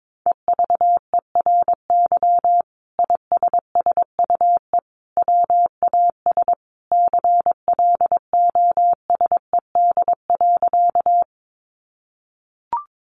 Morse code, heard as a keyed tone, about 22 wpm, 700 Hz. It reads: EVERY ISSUE WAS CLOSED.
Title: Sentences from Top 1400 Words 22wpm